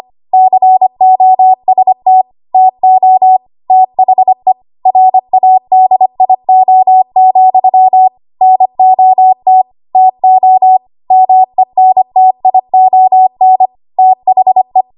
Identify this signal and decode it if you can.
Morse code, heard as a keyed tone, about 25 wpm, 750 Hz. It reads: COST TO THE RADIO, NOT TO MENTION THE
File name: Archive / 25 WPM / 25WPM